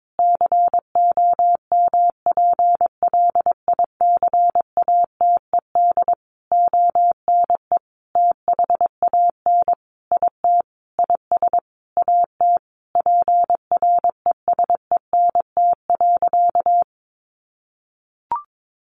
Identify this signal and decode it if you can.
Morse code, heard as a keyed tone, about 22 wpm, 700 Hz. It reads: COMPLICATED ONE THAN IT IS AT PRESENT.
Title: Sentences from Top 1200 Words 22wpm